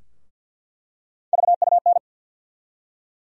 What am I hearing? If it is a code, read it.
Morse code, heard as a keyed tone, about 53 wpm, 700 Hz. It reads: VUN
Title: USA Call Sign Postfixes - 3 Letters 50wpm